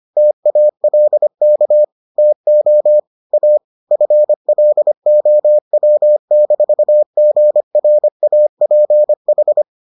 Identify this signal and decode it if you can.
Morse code, heard as a keyed tone, about 25 wpm, 600 Hz. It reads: TALK TO A FLOW-GRAPH